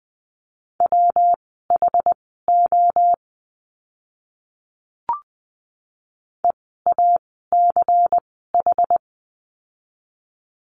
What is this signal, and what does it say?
Morse code, heard as a keyed tone, about 20 wpm, spaced as for 14 wpm, 700 Hz. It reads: WHO EACH
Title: 192 Lesson 31f - Letter v - Words 20wpm